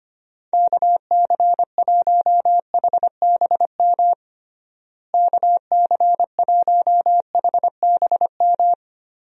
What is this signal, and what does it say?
Morse code, heard as a keyed tone, about 25 wpm, 700 Hz. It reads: KC1HBM KC1HBM
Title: USA Call Signs - 6 Characters - Repeated 3x - 25wpm